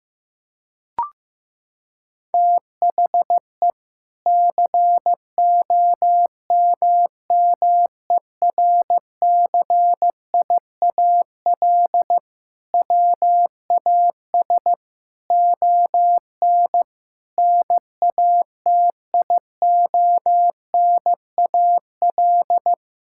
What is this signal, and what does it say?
Morse code, heard as a keyed tone, about 15 wpm, 700 Hz. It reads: THE COMMERCIAL WAS ON NATIONAL
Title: Sentences from Top 1000 Words - Review Part 5 - 15wpm